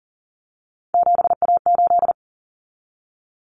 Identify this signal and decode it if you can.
Morse code, heard as a keyed tone, about 40 wpm, 700 Hz. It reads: 7A8
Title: Common Cut Numbers - 3-Digit 40wpm